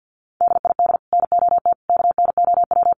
Morse code, heard as a keyed tone, about 50 wpm, 700 Hz. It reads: DID NOT KNOW
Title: Sets of 5 Words 50wpm